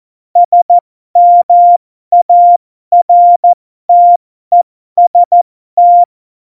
Morse code, heard as a keyed tone, about 14 wpm, spaced as for 12 wpm, 700 Hz. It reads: SMARTEST